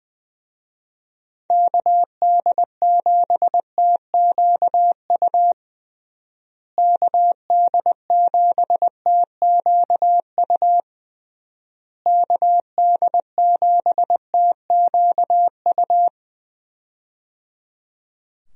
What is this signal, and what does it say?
Morse code, heard as a keyed tone, about 20 wpm, 700 Hz. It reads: KD7TQU KD7TQU KD7TQU
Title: USA Call Signs - 6 Characters - Repeated 3x - 20wpm